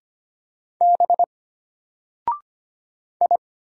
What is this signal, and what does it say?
Morse code, heard as a keyed tone, about 25 wpm, 700 Hz. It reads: B I
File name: Single-Letter-Number-25wpm